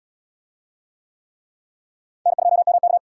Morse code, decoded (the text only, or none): N2MK